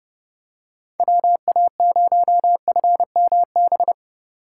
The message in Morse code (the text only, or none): WA0FMB